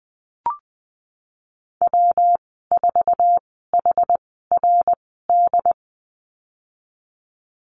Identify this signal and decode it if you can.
Morse code, heard as a keyed tone, about 20 wpm, spaced as for 14 wpm, 700 Hz. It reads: W4HRD
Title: 154 Lesson 25g - Letter p - Call Signs 20wpm